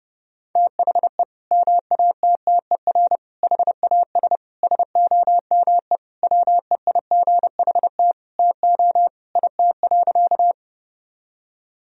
Morse code, heard as a keyed tone, about 30 wpm, 700 Hz. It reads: THE MATTER HAS SOME WEIGHT TO IT.
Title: Sentences from Top 1000 Words - Review Part 3 - 30wpm